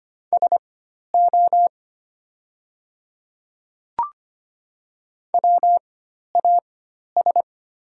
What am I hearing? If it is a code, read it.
Morse code, heard as a keyed tone, about 25 wpm, spaced as for 12 wpm, 700 Hz. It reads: SO WAS
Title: 4x Character Spacing - Top 100 Words - 25wpm